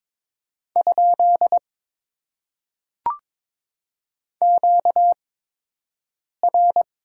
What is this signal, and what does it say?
Morse code, heard as a keyed tone, about 22 wpm, spaced as for 6 wpm, 700 Hz. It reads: ? QR
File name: Top 90 QSO Elements - 8x Character Spacing - 22wpm